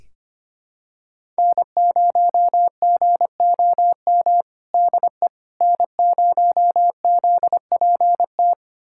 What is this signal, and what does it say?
Morse code, heard as a keyed tone, about 25 wpm, 700 Hz. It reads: N0GOM DE N0ZPT
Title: USA Callsign DE Callsign - 5 Characters - 25wpm